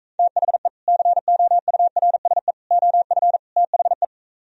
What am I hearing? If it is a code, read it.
Morse code, heard as a keyed tone, about 44 wpm, 700 Hz. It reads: THE COURSE OF THE